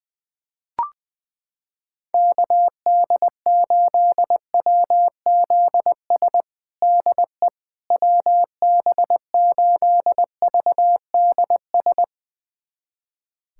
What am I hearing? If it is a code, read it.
Morse code, heard as a keyed tone, about 20 wpm, 700 Hz. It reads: KD8WZS DE WB8VDS
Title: USA Callsign DE Callsign - 6 Characters - 20wpm